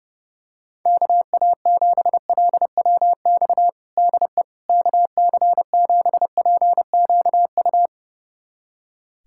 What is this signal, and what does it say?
Morse code, heard as a keyed tone, about 30 wpm, 700 Hz. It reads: KA7LWX DE KC7PQU
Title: USA Callsign DE Callsign - 6 Characters - 30wpm